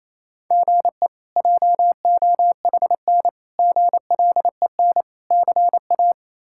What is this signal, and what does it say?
Morse code, heard as a keyed tone, about 28 wpm, 700 Hz. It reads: GE JOHN GLEN CA
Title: K1USN SST QSO Practice - 28wpm